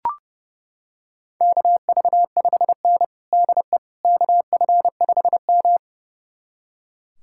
Morse code, heard as a keyed tone, about 30 wpm, 700 Hz. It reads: KV5N DE KF5M
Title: USA Callsign DE Callsign - 4 Characters - 30wpm